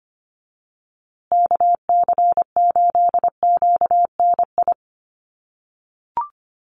Morse code, heard as a keyed tone, about 25 wpm, 700 Hz. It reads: KC8QNI
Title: USA Call Signs Repeated 2x 25wpm